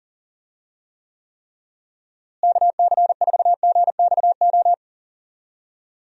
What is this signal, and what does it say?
Morse code, heard as a keyed tone, about 37 wpm, 700 Hz. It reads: KC4GXO
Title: USA Call Signs - 6 Characters - Repeated 3x - 40wpm